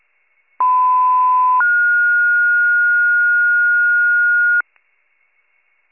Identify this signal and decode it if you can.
Two-tone page, heard as a one-second tone, then a three-second tone, about 1010 then 1430 Hz. It is heard from 0.6 to 4.6 s.